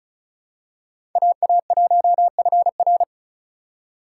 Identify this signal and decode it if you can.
Morse code, heard as a keyed tone, about 36 wpm, 700 Hz. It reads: AA1FR
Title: USA Call Signs - 5 Characters - 6x Character Spacing - 35wpm